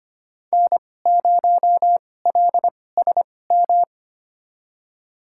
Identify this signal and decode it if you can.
Morse code, heard as a keyed tone, about 25 wpm, spaced as for 18 wpm, 700 Hz. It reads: N0LSM
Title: USA Call Signs - 5 Characters - 2x Character Spacing - 25wpm